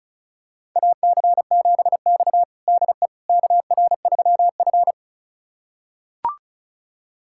Morse code, heard as a keyed tone, about 36 wpm, 700 Hz. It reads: AC7X DE KR3F
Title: USA Callsign DE Callsign - 4 Characters - 35wpm